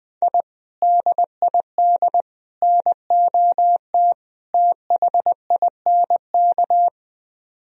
Morse code, heard as a keyed tone, about 20 wpm, 700 Hz. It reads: I DID NOT THINK